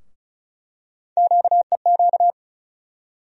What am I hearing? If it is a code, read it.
Morse code, heard as a keyed tone, about 34 wpm, 700 Hz. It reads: QEQ
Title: USA Call Sign Postfixes - 3 Letters 35wpm